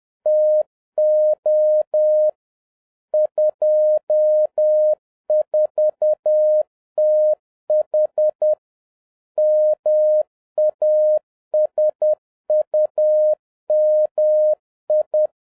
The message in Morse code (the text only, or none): TO 24TH MASUMI